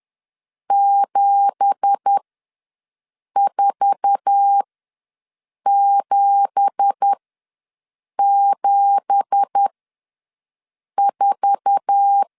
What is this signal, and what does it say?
Morse code, heard as a keyed tone, about 10 wpm, spaced as for 6 wpm, 800 Hz. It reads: 74774
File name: LEZIONI / 9 2867 ws 10wpm 4215